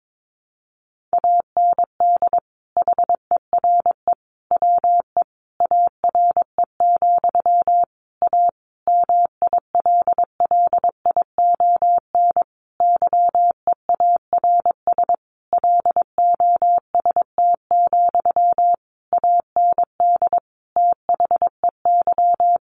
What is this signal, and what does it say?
Morse code, heard as a keyed tone, about 22 wpm, 700 Hz. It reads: AND HERE WE ARE, A MILLION YEARS LOST, AND THEY
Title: A Sound Of Thunder 22wpm